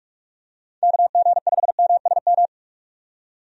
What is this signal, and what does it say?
Morse code, heard as a keyed tone, about 47 wpm, 700 Hz. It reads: KG5MSM
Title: USA Call Signs - 6 Characters - 45wpm